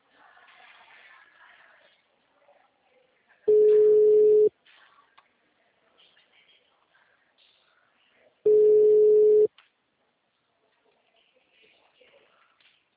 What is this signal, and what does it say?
Ringback tone (Europe, 425 Hz).